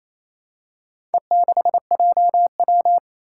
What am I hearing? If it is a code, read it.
Morse code, heard as a keyed tone, about 27 wpm, 700 Hz. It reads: E6JW
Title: Four Challenging Letter-Number Combinations 28wpm